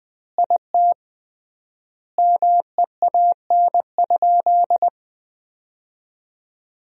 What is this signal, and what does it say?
Morse code, heard as a keyed tone, about 20 wpm, 700 Hz. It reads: IT MEAN?
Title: 3x Word-Spacing - Sentences from Top 200 Words - 20wpm